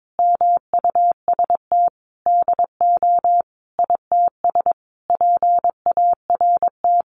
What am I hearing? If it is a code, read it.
Morse code, heard as a keyed tone, about 22 wpm, 700 Hz. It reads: MUST DO ITS PART
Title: Sentences from Top 1100 Words 22wpm